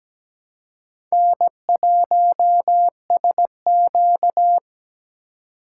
Morse code, heard as a keyed tone, about 20 wpm, 700 Hz. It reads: N1SQ
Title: Speed Racing - USA 1x2 Call Signs 17wpm